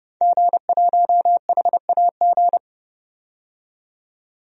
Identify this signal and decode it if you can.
Morse code, heard as a keyed tone, about 30 wpm, 700 Hz. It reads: G1HAG
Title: DXCC-100 Call Signs - 5 Characters - 30wpm